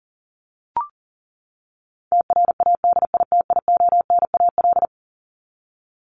Morse code, heard as a keyed tone, about 40 wpm, 700 Hz. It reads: TRADITIONAL
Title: Top 751-1000 Words 40wpm